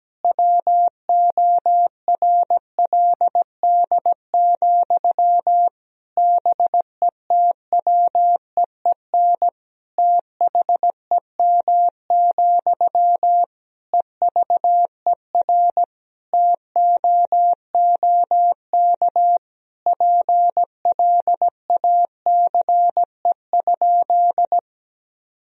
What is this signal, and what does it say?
Morse code, heard as a keyed tone, about 17 wpm, 700 Hz. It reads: WORLD, BETWEEN THEM, EVER TOOK PLACE?
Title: Sentences from Top 1000 Words - Review Part 1 - 17wpm